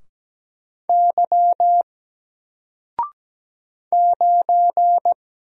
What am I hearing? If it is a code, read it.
Morse code, heard as a keyed tone, about 17 wpm, 700 Hz. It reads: Y 9
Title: Single Letter-Number Alt 17wpm